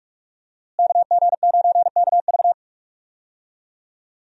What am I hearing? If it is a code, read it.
Morse code, heard as a keyed tone, about 46 wpm, 700 Hz. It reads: KG9KV
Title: USA Call Signs - 5 Characters - Repeated 2x - 45wpm